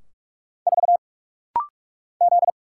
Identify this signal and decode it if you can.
Morse code, heard as a keyed tone, about 44 wpm, 700 Hz. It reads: 4 Z
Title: Single Letter-Number Alt - Rapid-Fire 45wpm